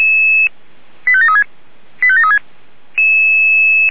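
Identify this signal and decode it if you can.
Five-tone selective call (ZVEI), heard as quick runs of five tones, tones of about 70 ms. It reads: 85736 85736
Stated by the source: Five-tone ZVEI selective call received on an SDR, 464 MHz.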